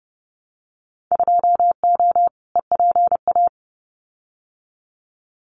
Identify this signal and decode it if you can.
Morse code, heard as a keyed tone, about 30 wpm, 700 Hz. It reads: 2O EPA
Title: Winter Field Day Exchanges Encore 30wpm